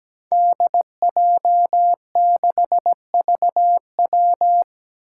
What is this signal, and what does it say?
Morse code, heard as a keyed tone, about 17 wpm, 700 Hz. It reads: DJ6VW
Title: Five Challenging Letter-Number Combinations 17wpm